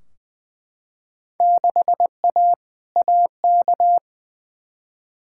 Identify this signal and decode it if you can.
Morse code, heard as a keyed tone, about 20 wpm, 700 Hz. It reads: 6A AK
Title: Field Day Exchanges Encore - 20wpm